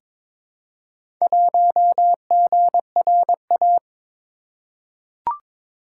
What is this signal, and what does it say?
Morse code, heard as a keyed tone, about 22 wpm, 700 Hz. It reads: 1GRA
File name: practice / 22wpm / Four Letter-Number Combinations - 22wpm